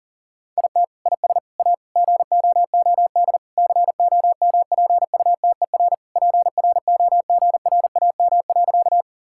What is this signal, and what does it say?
Morse code, heard as a keyed tone, about 37 wpm, 700 Hz. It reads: IT IS A GOOD COMPUTER PROGRAM.
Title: Sentences from Top 1000 Words - Review Part 5 - 40wpm